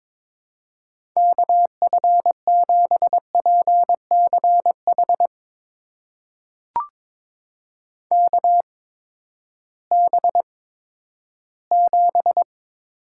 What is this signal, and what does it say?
Morse code, heard as a keyed tone, about 22 wpm, spaced as for 12 wpm, 700 Hz. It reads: KF7PCH KB7
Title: USA Call Signs - 6 Characters - 8x Character Spacing - 22wpm